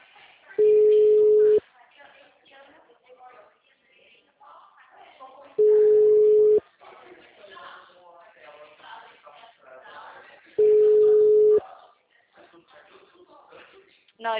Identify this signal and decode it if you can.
Ringback tone (Europe, 425 Hz), with speech.